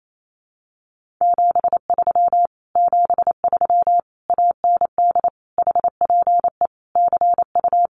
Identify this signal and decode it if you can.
Morse code, heard as a keyed tone, about 28 wpm, 700 Hz. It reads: 73 73 AND HPE CU
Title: Common QSO Phrases Encore 28wpm